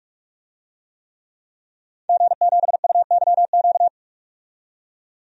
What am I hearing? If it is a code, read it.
Morse code, heard as a keyed tone, about 47 wpm, 700 Hz. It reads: G7UYQ
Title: DXCC-100 Call Signs - 5 Characters - 45wpm